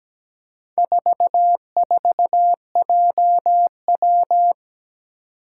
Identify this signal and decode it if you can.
Morse code, heard as a keyed tone, about 17 wpm, 700 Hz. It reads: 44JW
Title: Four Challenging Letter-Number Combinations 17wpm